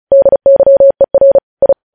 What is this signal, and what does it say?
Morse code, heard as a keyed tone, about 36 wpm, 550 Hz. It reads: DYER I
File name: podcast / News headlines-1x-35x35